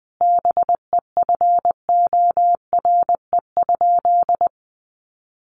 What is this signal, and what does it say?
Morse code, heard as a keyed tone, about 20 wpm, 700 Hz. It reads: BEFORE?
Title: Sentences from Top 900 Words 20wpm